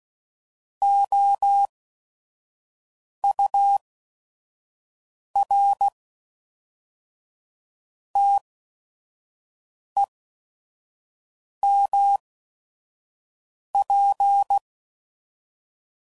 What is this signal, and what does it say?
Morse code, heard as a keyed tone, about 16 wpm, spaced as for 5 wpm, 800 Hz. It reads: OUR TEMP
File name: CW Code Groups @5WPM / QSOs / QSO WD0FBT